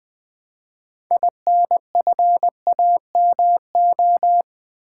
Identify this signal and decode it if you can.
Morse code, heard as a keyed tone, about 20 wpm, 700 Hz. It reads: INFAMO
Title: Build A Word - 3 Letters At A Time - 8-Letter Words - 20wpm